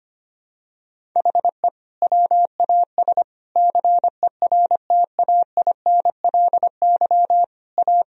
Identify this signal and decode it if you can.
Morse code, heard as a keyed tone, about 25 wpm, 700 Hz. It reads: HE WAS CERTAINLY A
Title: Sentences from Top 1000 Words - Review Part 1 - 25wpm